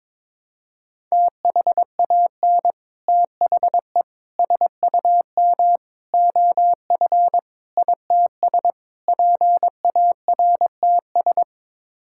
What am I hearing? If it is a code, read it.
Morse code, heard as a keyed tone, about 22 wpm, 700 Hz. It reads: THAN THE SUM OF ITS PARTS
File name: practice / 22wpm / Sets of 6 Words - 22wpm